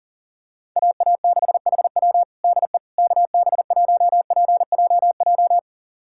Morse code, heard as a keyed tone, about 40 wpm, 700 Hz. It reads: AA6HW DE KB1PJJ
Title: USA Callsign DE Callsign 40wpm